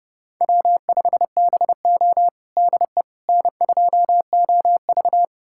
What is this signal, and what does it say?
Morse code, heard as a keyed tone, about 30 wpm, 700 Hz. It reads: W5BO DE N2OV
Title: USA Callsign DE Callsign - 4 Characters - 30wpm